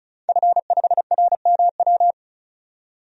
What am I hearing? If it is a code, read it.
Morse code, heard as a keyed tone, about 36 wpm, 700 Hz. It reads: F5RMW
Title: DXCC-100 Call Signs - 5 Characters - 35wpm